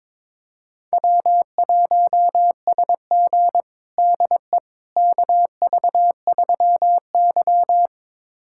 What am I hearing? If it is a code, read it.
Morse code, heard as a keyed tone, about 22 wpm, 700 Hz. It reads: W1SG DE KV3Y
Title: USA Callsign DE Callsign - 4 Characters - 22wpm